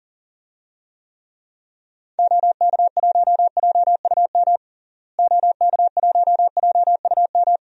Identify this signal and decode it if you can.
Morse code, heard as a keyed tone, about 40 wpm, 700 Hz. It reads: OK1JUM OK1JUM
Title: DXCC-100 Call Signs - 6 Characters - Repeated 2x - 40wpm